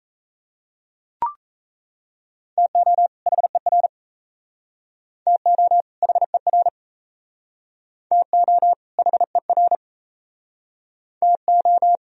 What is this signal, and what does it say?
Morse code, heard as a keyed tone, about 34 wpm, 700 Hz. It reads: TO HER TO HER TO HER TO
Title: Speed-Racing - Sets of 2 Words 28wpm